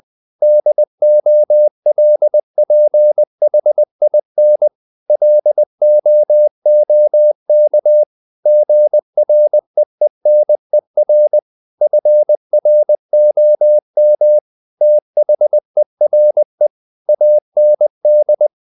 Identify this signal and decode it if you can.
Morse code, heard as a keyed tone, about 20 wpm, 600 Hz. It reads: DOLPHIN LOOK GREENER FROM THERE AND
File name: the old man and the sea part 2 20wpm